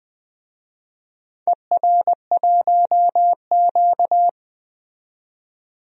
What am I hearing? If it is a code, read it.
Morse code, heard as a keyed tone, about 20 wpm, 700 Hz. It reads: ER1Q
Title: DXCC-100 Call Signs - 4 Characters - 20wpm